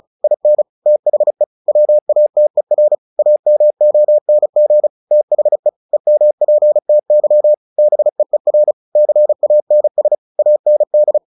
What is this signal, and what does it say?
Morse code, heard as a keyed tone, about 34 wpm, 600 Hz. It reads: IN THE WATER AMONG THE EMPTY BEER CANS AND
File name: the old man and the sea 2 35wpm